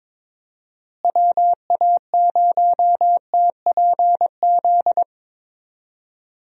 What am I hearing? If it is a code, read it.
Morse code, heard as a keyed tone, about 22 wpm, 700 Hz. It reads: WA0TPZ
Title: USA Call Signs - 6 Characters - Repeated 2x - 22wpm